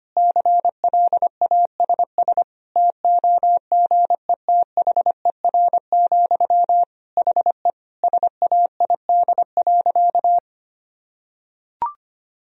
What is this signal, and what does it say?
Morse code, heard as a keyed tone, about 25 wpm, 700 Hz. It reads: CLASS TOGETHER, HE SAID.
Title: Sentences from Top 1000 Words - Review Part 2 - 25wpm